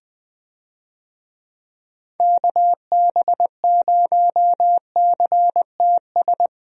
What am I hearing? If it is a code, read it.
Morse code, heard as a keyed tone, about 20 wpm, 700 Hz. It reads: KB0CTS
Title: USA Call Signs - 6 Characters - 20wpm